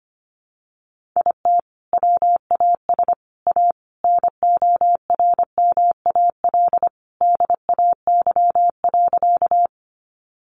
Morse code, heard as a keyed tone, about 25 wpm, 700 Hz. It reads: IT WAS A NORMAL DAY.
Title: Sentences from Top 1200 Words 25wpm